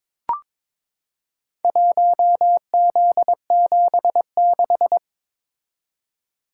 Morse code, heard as a keyed tone, about 22 wpm, 700 Hz. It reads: 1Z76
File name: practice / 22wpm / Four Letter-Number Combinations - 22wpm